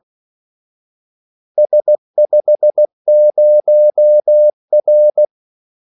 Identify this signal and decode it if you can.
Morse code, heard as a keyed tone, about 16 wpm, 600 Hz. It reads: S50R
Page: Kaldesignaler 16 wpm